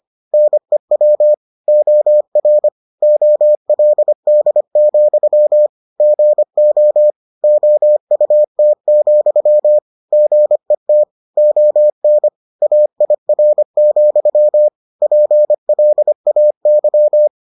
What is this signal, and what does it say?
Morse code, heard as a keyed tone, about 25 wpm, 600 Hz. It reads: NEW OR OLD, GO OUT, GET ON AIR, PLAY